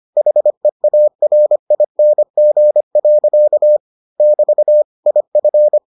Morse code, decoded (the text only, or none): HEARING. <BT> IF